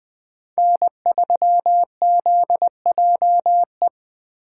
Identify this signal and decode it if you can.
Morse code, heard as a keyed tone, about 20 wpm, 700 Hz. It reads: N3ZJE
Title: USA Call Signs - 5 Characters - 2x Character Spacing - 20wpm